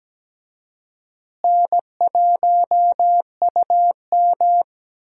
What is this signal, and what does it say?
Morse code, heard as a keyed tone, about 17 wpm, 700 Hz. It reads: N1UM
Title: USA Call Signs - 4 Characters - Repeated 2x - 17wpm